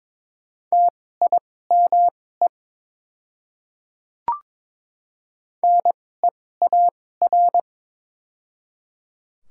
Morse code, heard as a keyed tone, about 22 wpm, spaced as for 16 wpm, 700 Hz. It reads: TIME NEAR
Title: 2x Character Spacing - Top 500 Words - 22wpm